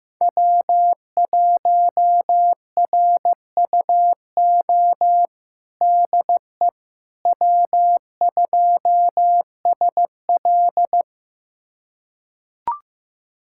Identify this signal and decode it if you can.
Morse code, heard as a keyed tone, about 15 wpm, 700 Hz. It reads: W1RUO DE W2SL
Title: USA Callsign DE Callsign 15wpm